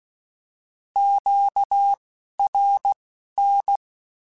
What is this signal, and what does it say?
Morse code, heard as a keyed tone, about 16 wpm, spaced as for 11 wpm, 800 Hz. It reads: QRN
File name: Misc / Misc-mp3s / Words @10